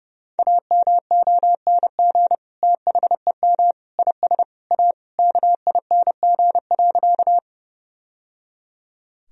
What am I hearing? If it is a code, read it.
Morse code, heard as a keyed tone, about 28 wpm, 700 Hz. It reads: AMONG THEM IS A KING.
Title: Sentences from Top 1000 Words - Review Part 3 - 30wpm